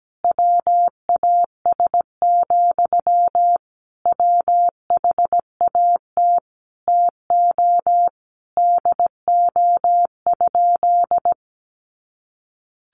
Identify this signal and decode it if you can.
Morse code, heard as a keyed tone, about 17 wpm, 700 Hz. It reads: WAS, WHAT TO DO?
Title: Sentences from Top 400 Words 17wpm